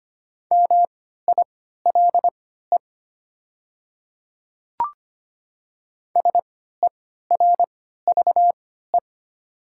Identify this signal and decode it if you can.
Morse code, heard as a keyed tone, about 25 wpm, spaced as for 14 wpm, 700 Hz. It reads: MILE SERVE
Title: 3x Character Spacing - Top 500 Words - 25wpm